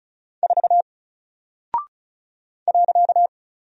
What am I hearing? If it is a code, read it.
Morse code, heard as a keyed tone, about 36 wpm, 700 Hz. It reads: 4 .
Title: Single Character (Letter:Number:Punctuation:Prosign) Alt 35wpm